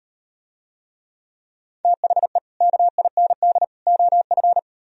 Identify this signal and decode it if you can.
Morse code, heard as a keyed tone, about 38 wpm, 700 Hz. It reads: THE KIND OF